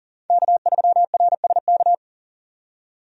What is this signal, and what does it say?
Morse code, heard as a keyed tone, about 40 wpm, 700 Hz. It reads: K3RSK
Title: USA Call Signs - 5 Characters - 40wpm